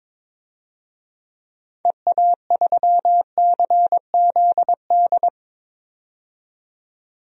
Morse code, heard as a keyed tone, about 22 wpm, 700 Hz. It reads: EA3CZD
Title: DXCC-100 Call Signs - 6 Characters - 22wpm